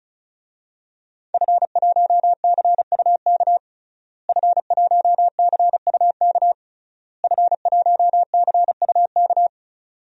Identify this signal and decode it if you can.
Morse code, heard as a keyed tone, about 36 wpm, 700 Hz. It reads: F1CUK F1CUK F1CUK
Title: DXCC-100 Call Signs - 5 Characters - Repeated 3x - 35wpm